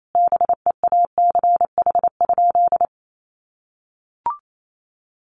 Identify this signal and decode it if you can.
Morse code, heard as a keyed tone, about 28 wpm, 700 Hz. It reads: BEACH?
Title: Sentences from Top 1500 Words 28wpm